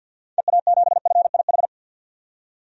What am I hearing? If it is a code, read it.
Morse code, heard as a keyed tone, about 52 wpm, 700 Hz. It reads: EA7FIH
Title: DXCC-100 Call Signs - 6 Characters - 50wpm